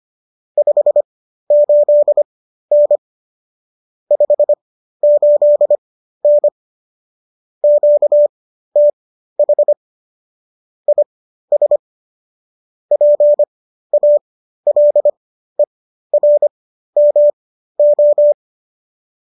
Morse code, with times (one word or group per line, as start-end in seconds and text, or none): 0.6-3.0 58N
4.1-6.5 58N
7.6-9.7 QTH
10.9-11.8 IS
12.9-18.3 PALERMO